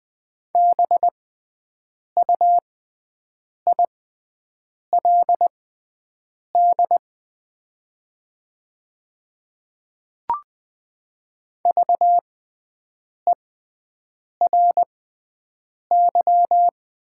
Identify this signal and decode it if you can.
Morse code, heard as a keyed tone, about 20 wpm, spaced as for 7 wpm, 700 Hz. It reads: BUILD VERY
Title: 6x Character Spacing - Top 200 Words - 20wpm